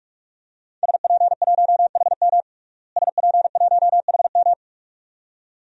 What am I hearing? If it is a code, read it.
Morse code, heard as a keyed tone, about 46 wpm, 700 Hz. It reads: SP1HM SP1HM
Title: DXCC-100 Call Signs - 5 Characters - Repeated 2x - 45wpm